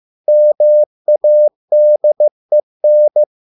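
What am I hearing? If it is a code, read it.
Morse code, heard as a keyed tone, about 15 wpm, 600 Hz. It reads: MADEN
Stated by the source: Morse code at a slow 15 wpm from CW med Gnister.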